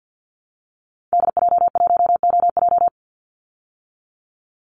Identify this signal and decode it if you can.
Morse code, heard as a keyed tone, about 50 wpm, 700 Hz. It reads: DJ1OJ
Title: Call Signs Repeated 2x - 50wpm